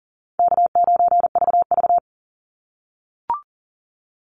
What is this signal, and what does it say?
Morse code, heard as a keyed tone, about 40 wpm, 700 Hz. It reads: K9VV
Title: Easy Call Signs - 40wpm